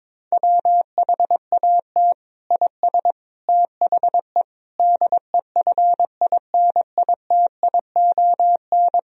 Morse code, heard as a keyed tone, about 22 wpm, 700 Hz. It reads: WHAT IS THE DEFINITION
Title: Sentences from Top 1000 Words - Review Part 4 - 22wpm